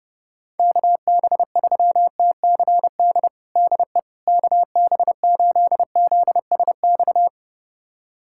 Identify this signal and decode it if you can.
Morse code, heard as a keyed tone, about 32 wpm, 700 Hz. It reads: KB3TCD DE KB8ZSX
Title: USA Callsign DE Callsign - 6 Characters - 30wpm